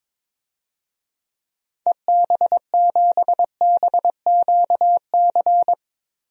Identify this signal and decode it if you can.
Morse code, heard as a keyed tone, about 22 wpm, 700 Hz. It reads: EB7BQC
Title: DXCC-100 Call Signs - 6 Characters - Repeated 2x - 22wpm